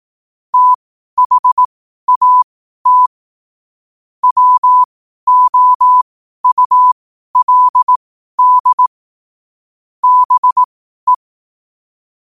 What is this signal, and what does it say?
Morse code, heard as a keyed tone, about 18 wpm, spaced as for 12 wpm, 1000 Hz. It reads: THAT WOULD BE
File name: qotdfiles / 12wpm